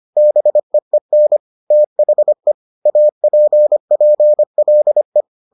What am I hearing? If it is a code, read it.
Morse code, heard as a keyed tone, about 25 wpm, 600 Hz. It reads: BEEN THE APPLE